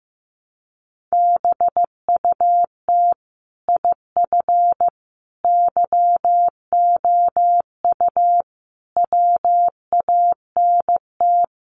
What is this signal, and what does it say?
Morse code, heard as a keyed tone, about 15 wpm, 700 Hz. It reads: BUT IF YOU WANT
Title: Sets of 5 Words Encore2 15wpm